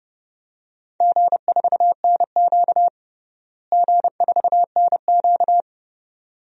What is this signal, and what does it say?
Morse code, heard as a keyed tone, about 30 wpm, 700 Hz. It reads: G4NQ G4NQ
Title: DXCC-100 Call Signs - 4 Characters - Repeated 2x - 30wpm